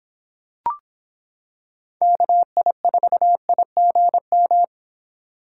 Morse code, heard as a keyed tone, about 25 wpm, 700 Hz. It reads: KI4IGM